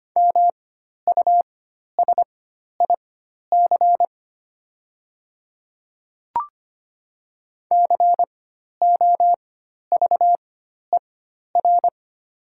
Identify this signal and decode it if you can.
Morse code, heard as a keyed tone, about 25 wpm, spaced as for 12 wpm, 700 Hz. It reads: MUSIC COVER
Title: 4x Character Spacing - Top 300 Words - 25wpm